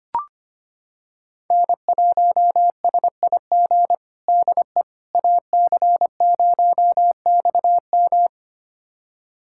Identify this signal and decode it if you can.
Morse code, heard as a keyed tone, about 25 wpm, 700 Hz. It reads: N1SIG DE AC0XM
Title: USA Callsign DE Callsign - 5 Characters - 25wpm